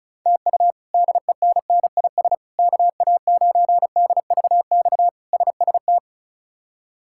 Morse code, heard as a keyed tone, about 34 wpm, 700 Hz. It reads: TU DENNIS KA9DVX SST
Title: K1USN SST QSO Practice - 35wpm